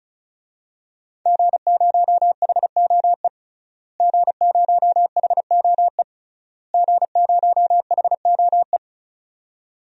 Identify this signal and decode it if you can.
Morse code, heard as a keyed tone, about 35 wpm, 700 Hz. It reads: G0HOE G0HOE G0HOE
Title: DXCC-100 Call Signs - 5 Characters - Repeated 3x - 35wpm